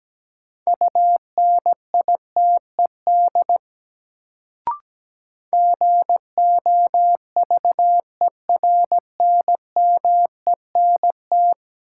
Morse code, heard as a keyed tone, about 17 wpm, 700 Hz. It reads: UNITED GOVERNMENT
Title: Sets of 2 Words - New Word Drill - 17wpm